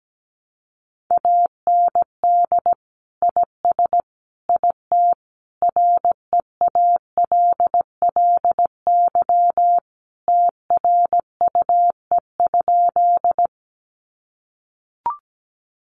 Morse code, heard as a keyed tone, about 17 wpm, 700 Hz. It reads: AND IS IT REALLY TRUE?
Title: Sentences from Top 600 Words 17wpm